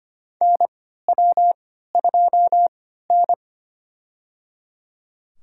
Morse code, heard as a keyed tone, about 25 wpm, spaced as for 14 wpm, 700 Hz. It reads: NW2N
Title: USA Call Signs - 3x Character Spacing - 25wpm